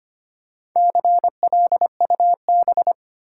Morse code, heard as a keyed tone, about 25 wpm, 700 Hz. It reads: CLUB
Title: Four Letter Words - 25wpm